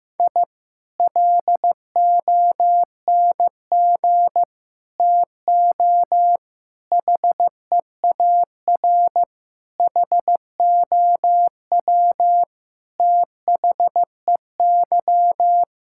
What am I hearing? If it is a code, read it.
Morse code, heard as a keyed tone, about 16 wpm, 700 Hz. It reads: I LONG TO HEAR HOW THEY
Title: Sentences from Top 1000 Words - Review Part 5 - 15wpm